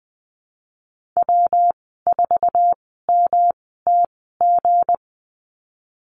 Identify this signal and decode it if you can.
Morse code, heard as a keyed tone, about 20 wpm, spaced as for 15 wpm, 700 Hz. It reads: W4MTG
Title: 167 Lesson 27g - Number 7 - Call Signs 20wpm